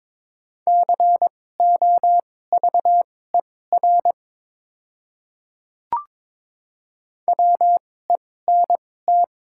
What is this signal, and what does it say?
Morse code, heard as a keyed tone, about 22 wpm, spaced as for 16 wpm, 700 Hz. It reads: COVER WENT
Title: 2x Character Spacing - Top 300 Words - 22wpm